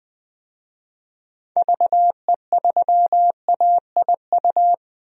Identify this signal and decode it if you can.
Morse code, heard as a keyed tone, about 20 wpm, 700 Hz. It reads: VE3AIU
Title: DXCC-100 Call Signs - 6 Characters - Repeated 2x - 20wpm